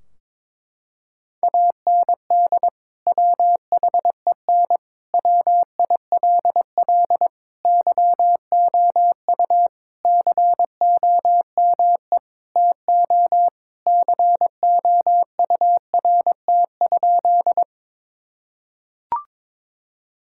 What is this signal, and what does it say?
Morse code, heard as a keyed tone, about 22 wpm, 700 Hz. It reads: AND WHEN WILL YOU COME TO COURT?
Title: Sentences from Top 1600 Words 22wpm